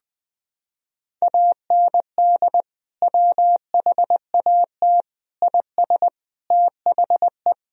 Morse code, heard as a keyed tone, about 20 wpm, 700 Hz. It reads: AND WHAT IS THE
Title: Sentences from Top 1000 Words - Review Part 3 - 20wpm